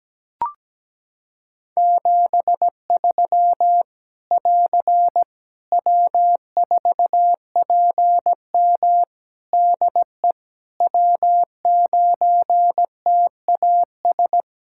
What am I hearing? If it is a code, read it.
Morse code, heard as a keyed tone, about 17 wpm, 700 Hz. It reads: 73 <AR> W4PM DE W9TAS
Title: K1USN SST QSO Practice 17wpm